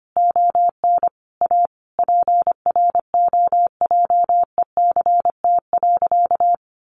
Morse code, heard as a keyed tone, about 25 wpm, 700 Hz. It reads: ON A PROJECT.
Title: Sentences from Top 1400 Words 25wpm